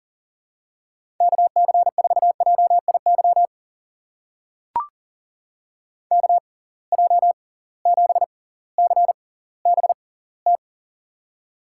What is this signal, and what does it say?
Morse code, heard as a keyed tone, about 40 wpm, spaced as for 20 wpm, 700 Hz. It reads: KC4JIY KJ7CBT
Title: USA Call Signs - 6 Characters - 6x Character Spacing - 40wpm